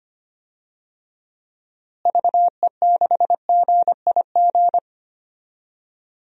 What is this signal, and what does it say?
Morse code, heard as a keyed tone, about 25 wpm, 700 Hz. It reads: VE6GIG
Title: DXCC-100 Call Signs - 6 Characters - Repeated 3x - 25wpm